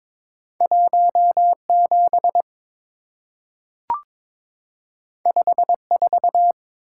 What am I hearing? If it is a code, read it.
Morse code, heard as a keyed tone, about 22 wpm, 700 Hz. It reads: 17 54
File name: practice / 22wpm / Two Letter-Number Combinations - 22wpm